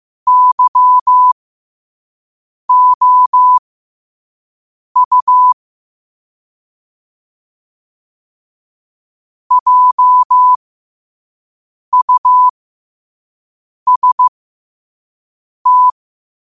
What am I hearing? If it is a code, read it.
Morse code, heard as a keyed tone, about 15 wpm, spaced as for 5 wpm, 1000 Hz. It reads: YOU JUST
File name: qotdfiles / 5wpm